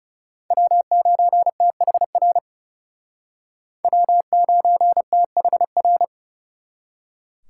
Morse code, heard as a keyed tone, about 37 wpm, 700 Hz. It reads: W9THR W9THR
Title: Speed Racing - USA 1x3 Call Signs 30wpm